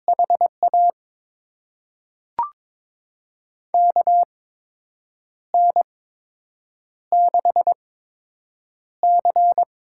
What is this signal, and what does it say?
Morse code, heard as a keyed tone, about 22 wpm, spaced as for 7 wpm, 700 Hz. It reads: HA KN6C
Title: USA Call Signs - 6 Characters - 8x Character Spacing - 22wpm